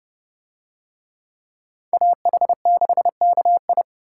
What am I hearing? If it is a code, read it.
Morse code, heard as a keyed tone, about 29 wpm, 700 Hz. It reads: AH6KI